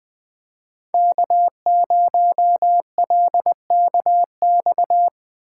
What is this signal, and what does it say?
Morse code, heard as a keyed tone, about 20 wpm, 700 Hz. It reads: K0LKX
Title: USA Call Signs - 5 Characters - 4x Character Spacing - 20wpm